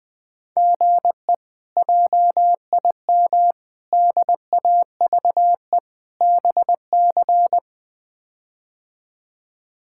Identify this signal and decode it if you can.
Morse code, heard as a keyed tone, about 21 wpm, 700 Hz. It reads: GE JIM DAVE BC
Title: K1USN SST QSO Practice - 20wpm